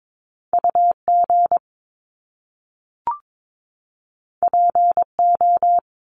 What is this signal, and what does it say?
Morse code, heard as a keyed tone, about 22 wpm, 700 Hz. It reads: UG PO
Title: Top Two Letter Combinations in Words 22wpm